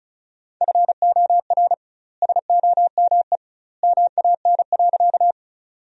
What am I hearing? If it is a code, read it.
Morse code, heard as a keyed tone, about 34 wpm, 700 Hz. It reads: FOR SOME MAN.
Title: 2x Word-Spacing - Sentences from Top 200 Words - 35wpm